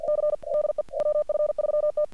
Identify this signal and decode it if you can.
Morse code, heard as a keyed tone, about 48 wpm, 1200 Hz, with noise. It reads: C DE WF3T
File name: archived / CRSnet